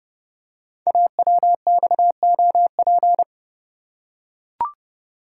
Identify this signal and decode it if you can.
Morse code, heard as a keyed tone, about 29 wpm, 700 Hz. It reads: AWXOP
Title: Code Groups - 5 Letters - 30wpm